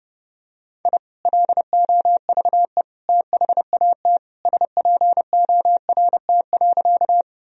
Morse code, heard as a keyed tone, about 30 wpm, 700 Hz. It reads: I LOVE THAT SPORT.
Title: Sentences from Top 1000 Words - Review Part 4 - 30wpm